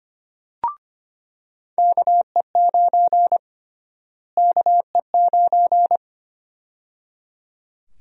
Morse code, heard as a keyed tone, about 25 wpm, 700 Hz. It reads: KE9 KE9
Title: USA Call Sign Prefixes - Repeated 2x - 25wpm